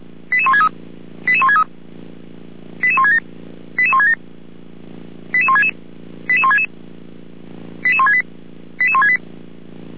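Five-tone selective call (ZVEI), heard as quick runs of five tones, tones of about 70 ms. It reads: 88163 88163 88167 88167 88166 88166 88168 88168
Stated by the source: ZVEI five-tone selective call received by SDR.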